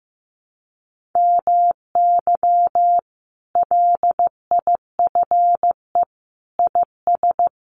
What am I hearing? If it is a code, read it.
Morse code, heard as a keyed tone, about 14 wpm, 700 Hz. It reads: MY LIFE IS
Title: Sentences from Top 1400 Words 15wpm